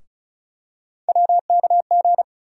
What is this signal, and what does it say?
Morse code, heard as a keyed tone, about 36 wpm, 700 Hz. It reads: WKG
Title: USA Call Sign Postfixes - 3 Letters 35wpm